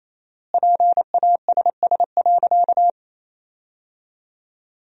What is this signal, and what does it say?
Morse code, heard as a keyed tone, about 27 wpm, 700 Hz. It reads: PASS.
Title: 4x Word-Spacing - Sentences from Top 400 Words - 28wpm